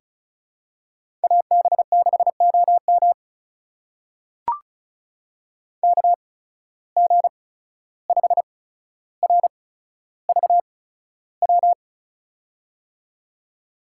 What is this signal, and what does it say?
Morse code, heard as a keyed tone, about 36 wpm, spaced as for 14 wpm, 700 Hz. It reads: AB6OM KG5RVW
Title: USA Call Signs - 8x Character Spacing - 35wpm